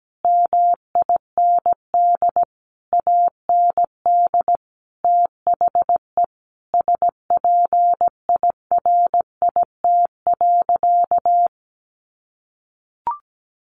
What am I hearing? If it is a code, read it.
Morse code, heard as a keyed tone, about 17 wpm, 700 Hz. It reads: MIND AND THE SPIRIT.
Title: Sentences from Top 1400 Words 17wpm